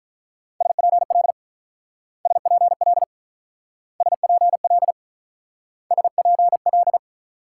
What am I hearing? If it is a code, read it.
Morse code, heard as a keyed tone, about 45 wpm, 700 Hz. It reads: SPL SPL SPL SPL